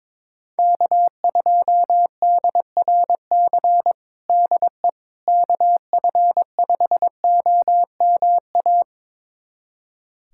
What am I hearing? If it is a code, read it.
Morse code, heard as a keyed tone, about 22 wpm, 700 Hz. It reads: K2DRC DE KF5OMA
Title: USA Callsign DE Callsign 22wpm